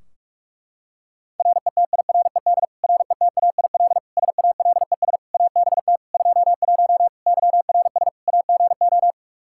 Morse code, heard as a keyed tone, about 46 wpm, 700 Hz. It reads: RETIRED RETAIL SALES ABT 21 YRS AGO
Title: Practice QSO - Exchange 3 - More Info - 45wpm